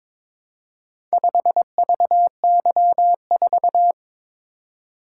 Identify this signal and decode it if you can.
Morse code, heard as a keyed tone, about 22 wpm, 700 Hz. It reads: 5VY4
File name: practice / 22wpm / Four Letter-Number Combinations - 22wpm